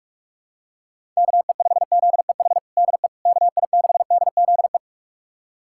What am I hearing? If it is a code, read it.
Morse code, heard as a keyed tone, about 41 wpm, 700 Hz. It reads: KE5ZEH DE KI6DZE
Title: USA Callsign DE Callsign 45wpm